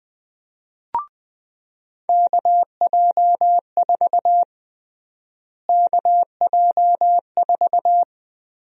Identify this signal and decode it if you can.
Morse code, heard as a keyed tone, about 21 wpm, 700 Hz. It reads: KJ4 KJ4
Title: USA Call Sign Prefixes - Repeated 2x - 20wpm